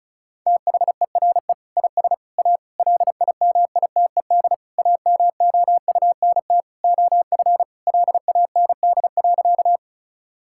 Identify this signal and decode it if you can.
Morse code, heard as a keyed tone, about 34 wpm, 700 Hz. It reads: THERE IS A LIMITED AMOUNT OF LAND.
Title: Sentences from Top 1000 Words - Review Part 6 - 35wpm